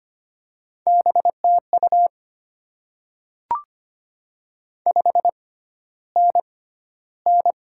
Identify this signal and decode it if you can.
Morse code, heard as a keyed tone, about 25 wpm, spaced as for 13 wpm, 700 Hz. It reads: BTU 5NN
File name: Top 20 QSO Elements - 6x Character Spacing - 25wpm